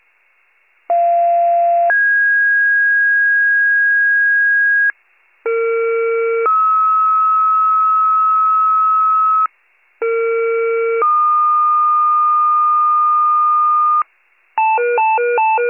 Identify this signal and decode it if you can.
Two-tone page, heard as a one-second tone, then a three-second tone, about 690 then 1650 Hz, about 460 then 1290 Hz, about 460 then 1220 Hz. It is heard from 0.9 to 14.0 s.